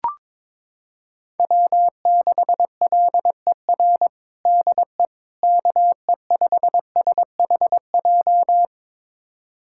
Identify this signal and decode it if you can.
Morse code, heard as a keyed tone, about 22 wpm, 700 Hz. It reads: W6LER DE KE5SHJ
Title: USA Callsign DE Callsign 22wpm